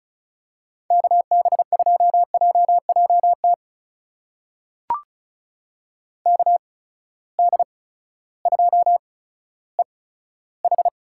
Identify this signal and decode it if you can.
Morse code, heard as a keyed tone, about 36 wpm, spaced as for 16 wpm, 700 Hz. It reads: KB2JJT KD2EH
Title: USA Call Signs - 6 Characters - 8x Character Spacing - 35wpm